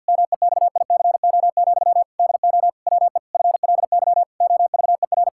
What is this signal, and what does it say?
Morse code, heard as a keyed tone, about 50 wpm, 700 Hz. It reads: MEXICO, DO WE FLY OVER
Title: Pangram Challenge v2 - 50wpm